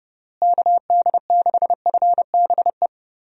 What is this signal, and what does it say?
Morse code, heard as a keyed tone, about 30 wpm, 700 Hz. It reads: KD6FBE
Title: USA Call Signs - 6 Characters - Repeated 2x - 30wpm